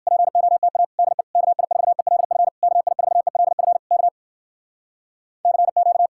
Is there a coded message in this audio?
Yes: Morse code from 0.1 to 6.1 s, heard as a keyed tone, about 59 wpm, 700 Hz. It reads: POTA DE KI4ELF KI4ELF K CQ